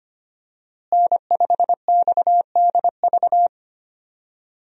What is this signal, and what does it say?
Morse code, heard as a keyed tone, about 25 wpm, 700 Hz. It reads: N5XDV
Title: USA Call Signs - 5 Characters - Repeated 2x - 25wpm